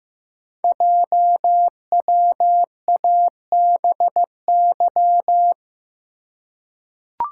Morse code, heard as a keyed tone, about 15 wpm, 700 Hz. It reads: JWABY
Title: Code Groups - 5 Letters - 15wpm